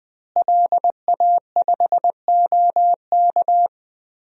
Morse code, heard as a keyed tone, about 20 wpm, 700 Hz. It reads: LA5OK
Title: DXCC-100 Call Signs - 5 Characters - Repeated 3x - 20wpm